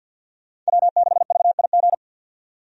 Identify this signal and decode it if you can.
Morse code, heard as a keyed tone, about 51 wpm, 700 Hz. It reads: W6VIG
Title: USA Call Signs - 5 Characters - 50wpm